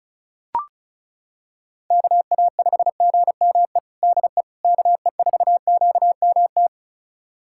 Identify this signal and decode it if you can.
Morse code, heard as a keyed tone, about 36 wpm, 700 Hz. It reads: KA5GME DE KE4QMT
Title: USA Callsign DE Callsign - 6 Characters - 35wpm